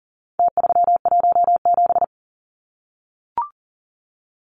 Morse code, heard as a keyed tone, about 40 wpm, 700 Hz. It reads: T317
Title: Numbers - Contest Style - 4-Digit 40wpm